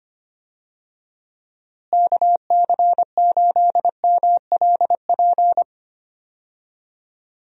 Morse code, heard as a keyed tone, about 25 wpm, 700 Hz. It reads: KC8MLP
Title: USA Call Signs - 6 Characters - Repeated 3x - 25wpm